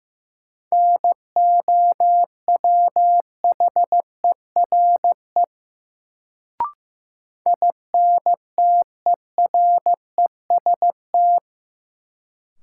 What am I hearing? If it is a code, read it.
Morse code, heard as a keyed tone, about 15 wpm, 700 Hz. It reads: NOWHERE INTEREST
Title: Sets of 5 Words Encore - New Word Drill - 15wpm